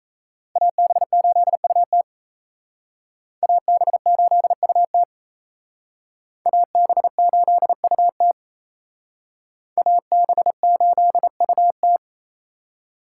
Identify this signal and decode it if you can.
Morse code, heard as a keyed tone, about 34 wpm, 700 Hz. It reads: AB8UT AB8UT AB8UT AB8UT
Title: Speed Racing - USA Call Signs 28wpm